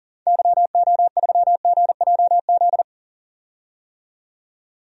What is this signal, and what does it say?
Morse code, heard as a keyed tone, about 37 wpm, 700 Hz. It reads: YO3GJZ
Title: DXCC-100 Call Signs - 6 Characters - 40wpm